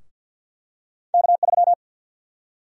Morse code, heard as a keyed tone, about 49 wpm, 700 Hz. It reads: K3
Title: USA Call Sign Prefixes - Repeated 2x - 50wpm